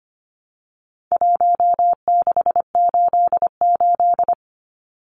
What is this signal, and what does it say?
Morse code, heard as a keyed tone, about 25 wpm, 700 Hz. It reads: 1688
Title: Numbers - Contest Style - 4-Digit 25wpm